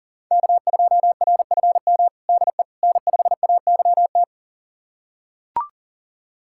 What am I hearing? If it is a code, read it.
Morse code, heard as a keyed tone, about 37 wpm, 700 Hz. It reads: K2RFM DE N5AYT
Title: USA Callsign DE Callsign - 5 Characters - 40wpm